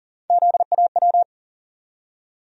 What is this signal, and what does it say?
Morse code, heard as a keyed tone, about 40 wpm, 700 Hz. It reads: ZAW
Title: USA Call Sign Postfixes - 3 Letters 40wpm